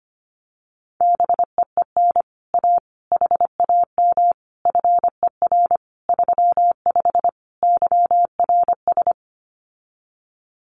Morse code, heard as a keyed tone, about 25 wpm, 700 Hz. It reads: BEEN A HAM FER 35 YRS
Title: Common QSO Phrases Encore 25wpm